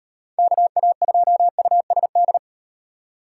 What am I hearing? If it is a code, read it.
Morse code, heard as a keyed tone, about 38 wpm, 700 Hz. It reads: KA2USD